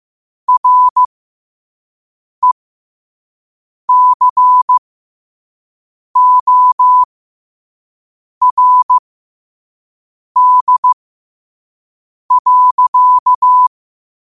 Morse code, heard as a keyed tone, about 15 wpm, spaced as for 5 wpm, 1000 Hz. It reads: RECORD.
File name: qotdfiles / 5wpm